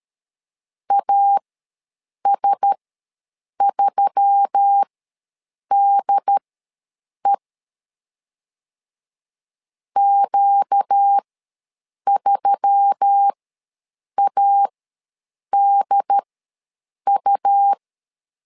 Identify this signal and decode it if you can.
Morse code, heard as a keyed tone, about 13 wpm, spaced as for 7 wpm, 800 Hz. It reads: AS3DE Q3ADU
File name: LEZIONI / 4 Q1S3 ws 12wpm 2038